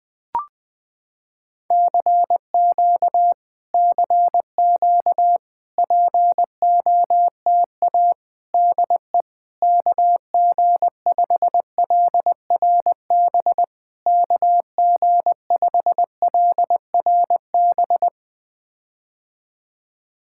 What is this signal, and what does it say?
Morse code, heard as a keyed tone, about 20 wpm, 700 Hz. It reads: CQ CQ POTA DE KG5LRB KG5LRB
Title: POTA QSO Practice - 20wpm